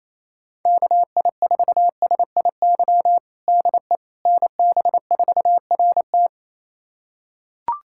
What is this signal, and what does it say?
Morse code, heard as a keyed tone, about 28 wpm, 700 Hz. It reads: KI4SIY DE NB4RT
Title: USA Callsign DE Callsign 28wpm